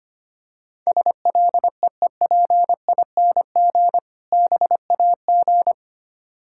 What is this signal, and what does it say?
Morse code, heard as a keyed tone, about 25 wpm, 700 Hz. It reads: SLEEPING BAG
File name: practice / 25wpm / Difficult Words with Hints - 25wpm